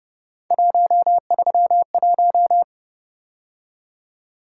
Morse code, heard as a keyed tone, about 29 wpm, 700 Hz. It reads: 131